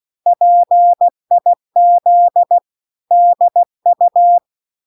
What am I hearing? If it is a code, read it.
Morse code, heard as a keyed tone, about 16 wpm, 700 Hz. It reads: PIZDU